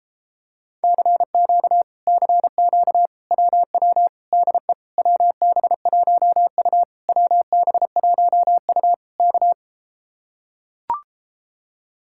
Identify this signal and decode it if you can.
Morse code, heard as a keyed tone, about 33 wpm, 700 Hz. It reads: CQ CQ WW DE WB1U WB1U K